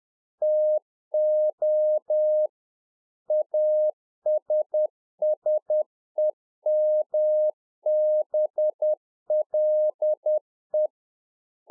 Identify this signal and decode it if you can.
Morse code, heard as a keyed tone, about 10 wpm, 610 Hz. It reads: TO ASSEMBLE
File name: 2011-04-mp3 / A SMALL TRANCEIVER 1 10WPM